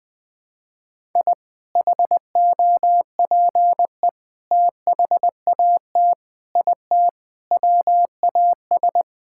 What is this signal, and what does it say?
Morse code, heard as a keyed tone, about 20 wpm, 700 Hz. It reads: I HOPE THAT IT WAS
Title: Sentences from Top 1000 Words - Review Part 5 - 20wpm